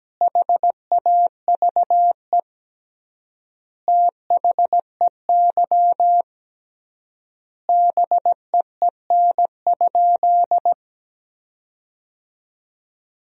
Morse code, heard as a keyed tone, about 17 wpm, 700 Hz. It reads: HAVE THEY BEEN?
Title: 3x Word-Spacing - Sentences from Top 200 Words - 17wpm